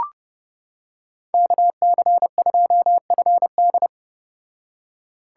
Morse code, heard as a keyed tone, about 29 wpm, 700 Hz. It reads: KC2FD